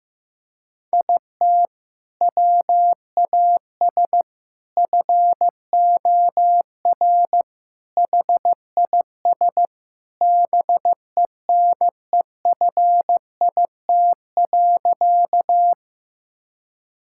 Morse code, heard as a keyed tone, about 15 wpm, 700 Hz. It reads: IT WAS FOR HIS BENEFIT.
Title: Sentences from Top 1000 Words - Review Part 6 - 15wpm